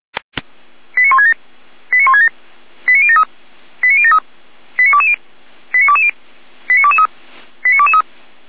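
Five-tone selective call (ZVEI), heard as quick runs of five tones, tones of about 70 ms. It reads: 89167 89167 89073 89073 89220 89220 89223 89223
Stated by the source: Five-tone ZVEI selective call received on SDR.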